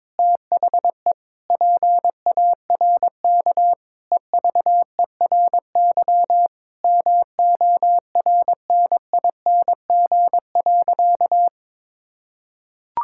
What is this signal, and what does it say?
Morse code, heard as a keyed tone, about 21 wpm, 700 Hz. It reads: THE PARK EVERY MORNING.
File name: practice / 22wpm / Sentences with 2-Word Phrases - 22wpm